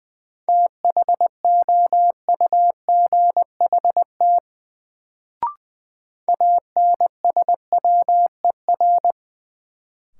Morse code, heard as a keyed tone, about 20 wpm, 700 Hz. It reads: THOUGHT ANSWER
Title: Sentences from Top 300 Words - New Word Drill - 20wpm